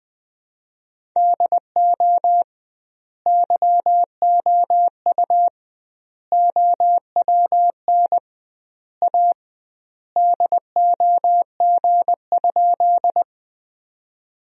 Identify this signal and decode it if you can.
Morse code, heard as a keyed tone, about 20 wpm, 700 Hz. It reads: DO YOU OWN A DOG?
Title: 2x Word-Spacing - Sentences from Top 400 Words - 20wpm